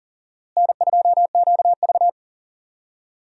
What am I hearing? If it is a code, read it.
Morse code, heard as a keyed tone, about 40 wpm, 700 Hz. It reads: N2QV
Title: USA Call Signs - 4 Characters - Repeated 2x - 40wpm